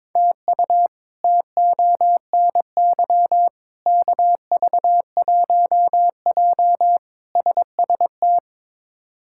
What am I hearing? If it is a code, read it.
Morse code, heard as a keyed tone, about 22 wpm, 700 Hz. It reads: TU TONY KV1J SST
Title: K1USN SST QSO Practice - 22wpm